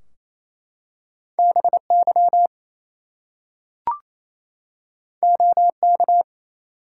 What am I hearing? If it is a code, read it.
Morse code, heard as a keyed tone, about 29 wpm, 700 Hz. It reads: BY OK
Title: Two Letter Words - 28wpm